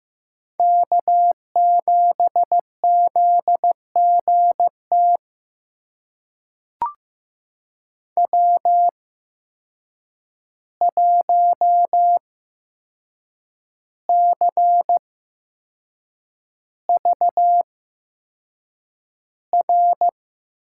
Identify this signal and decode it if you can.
Morse code, heard as a keyed tone, about 15 wpm, spaced as for 6 wpm, 700 Hz. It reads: K7ZGT W1CVR
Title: USA Call Signs - 5 Characters - 8x Character Spacing - 15wpm